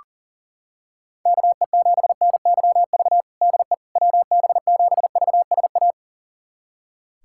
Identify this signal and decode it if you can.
Morse code, heard as a keyed tone, about 40 wpm, 700 Hz. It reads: KE7NYV DE WB7VSA
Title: USA Callsign DE Callsign 40wpm